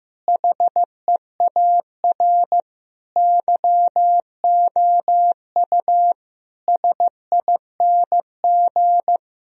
Morse code, heard as a keyed tone, about 15 wpm, 700 Hz. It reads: HEAR YOU SING
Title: Sentences from Top 1000 Words - Review Part 4 - 15wpm